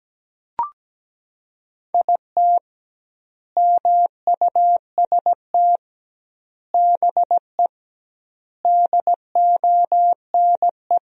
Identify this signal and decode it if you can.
Morse code, heard as a keyed tone, about 17 wpm, 700 Hz. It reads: IT MUST BE DONE
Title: 2x Word-Spacing - Sentences from Top 500 Words - 17wpm